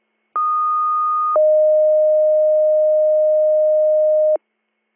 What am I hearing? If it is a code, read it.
Two-tone page, heard as a one-second tone, then a three-second tone, about 1230 then 620 Hz. It is heard from 0.4 to 4.4 s.